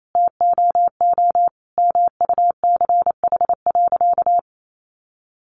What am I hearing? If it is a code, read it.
Morse code, heard as a keyed tone, about 28 wpm, 700 Hz. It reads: TOO MUCH.
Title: Sentences from Top 400 Words 28wpm